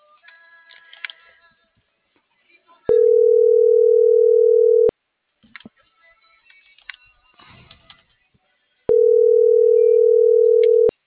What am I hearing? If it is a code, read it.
Ringback tone (North America), 440 + 480 Hz.